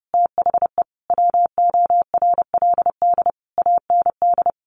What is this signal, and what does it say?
Morse code, heard as a keyed tone, about 28 wpm, 700 Hz. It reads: THE WORLD AND
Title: Sentences from Top 1300 Words 30wpm